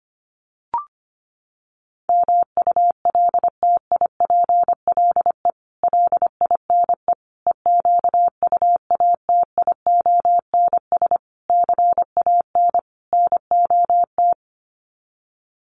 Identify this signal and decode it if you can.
Morse code, heard as a keyed tone, about 25 wpm, 700 Hz. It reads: MULTIPLE LINE EQUATIONS CAN NOT
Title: Sets of 5 Words 25wpm